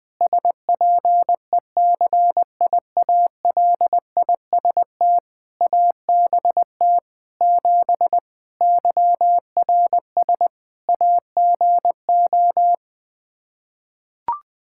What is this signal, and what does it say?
Morse code, heard as a keyed tone, about 20 wpm, 700 Hz. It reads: SPECIALIST ABT 7 YRS AGO
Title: Practice QSO - Exchange 3 - More Info - 20wpm